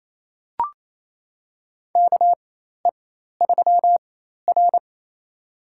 Morse code, heard as a keyed tone, about 28 wpm, spaced as for 13 wpm, 700 Hz. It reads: KE3R
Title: USA Call Signs - 4 Characters - 4x Character Spacing - 28wpm